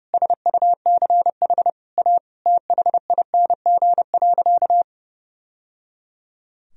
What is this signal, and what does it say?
Morse code, heard as a keyed tone, about 30 wpm, 700 Hz. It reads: SUCH A THING.
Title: Sentences from Top 1000 Words - Review Part 1 - 30wpm